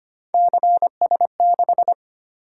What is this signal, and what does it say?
Morse code, heard as a keyed tone, about 25 wpm, 700 Hz. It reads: CS6
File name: Three Letter-Number Combinations - 25wpm